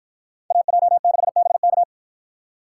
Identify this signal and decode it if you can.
Morse code, heard as a keyed tone, about 53 wpm, 700 Hz. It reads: AJ6BK